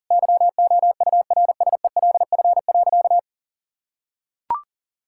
Morse code, heard as a keyed tone, about 40 wpm, 700 Hz. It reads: YOURSELF.
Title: Sentences from Top 1000 Words - Review Part 5 - 40wpm